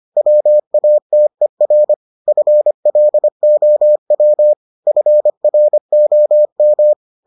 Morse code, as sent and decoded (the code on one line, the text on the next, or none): .-- .- - . .-. / ..-. .-.. --- .-- / ..-. .-. --- --
WATER FLOW FROM